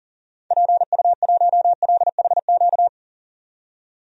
Morse code, heard as a keyed tone, about 37 wpm, 700 Hz. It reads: PU1LHQ
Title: DXCC-100 Call Signs - 6 Characters - 40wpm